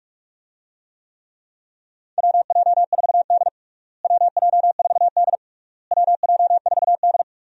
Morse code, heard as a keyed tone, about 46 wpm, 700 Hz. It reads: WJ4D WJ4D WJ4D
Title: USA Call Signs - 4 Characters - Repeated 3x - 45wpm